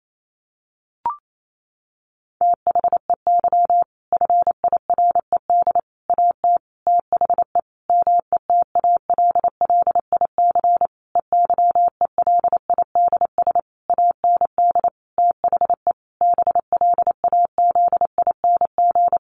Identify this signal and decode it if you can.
Morse code, heard as a keyed tone, about 28 wpm, 700 Hz. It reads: THEY FIRED AT THE METALLIC EYELIDS AND THE BLAZING
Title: A Sound Of Thunder 28wpm